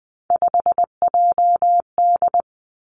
Morse code, heard as a keyed tone, about 19 wpm, 700 Hz. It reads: 5JD
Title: Three Challenging Letter-Number Combinations 20wpm